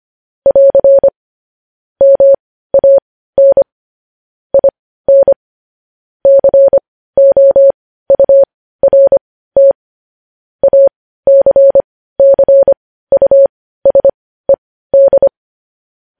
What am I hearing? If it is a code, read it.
Morse code, heard as a keyed tone, about 25 wpm, spaced as for 15 wpm, 550 Hz. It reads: <AR> MAN IN COURT ACCUSED
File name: podcast / News headlines-1x-25x15